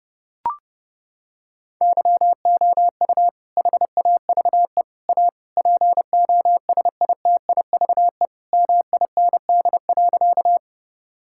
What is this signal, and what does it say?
Morse code, heard as a keyed tone, about 32 wpm, 700 Hz. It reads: YOU HAVE A POSITIVE MIND.
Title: Sentences from Top 1000 Words - Review Part 5 - 30wpm